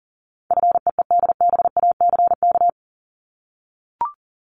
Morse code, heard as a keyed tone, about 37 wpm, 700 Hz. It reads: FEEDBACK
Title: Top 1501-1750 Words 40wpm